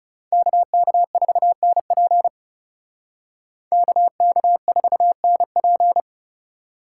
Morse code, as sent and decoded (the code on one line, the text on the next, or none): -.- -.- ....- -. .--. / -.- -.- ....- -. .--.
KK4NP KK4NP